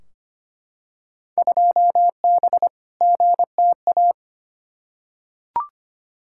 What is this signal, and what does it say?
Morse code, heard as a keyed tone, about 25 wpm, 700 Hz. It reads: 2B GTA
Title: Field Day Exchanges Encore - 25wpm